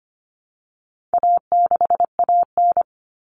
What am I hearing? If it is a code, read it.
Morse code, heard as a keyed tone, about 25 wpm, 700 Hz. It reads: A6AN
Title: Common Cut Numbers - 4-Digit 25wpm